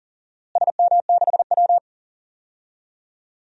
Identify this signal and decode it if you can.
Morse code, heard as a keyed tone, about 40 wpm, 700 Hz. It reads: SM6W
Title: DXCC-100 Call Signs - 4 Characters - Repeated 3x - 40wpm